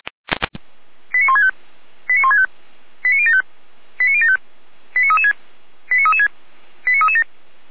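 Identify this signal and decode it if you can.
Five-tone selective call (ZVEI), heard as quick runs of five tones, tones of about 70 ms. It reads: 89165 89165 89075 89075 89336 89336 89337
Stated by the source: ZVEI five-tone call 89220 received on SDR, from feuerwehr-orlamuende.org (Orlamünde fire brigade).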